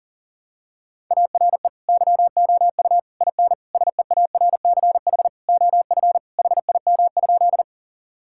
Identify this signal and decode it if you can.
Morse code, heard as a keyed tone, about 38 wpm, 700 Hz. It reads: ARE YOU IN SEARCH OF HIM?
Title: Sentences from Top 1000 Words - Review Part 5 - 40wpm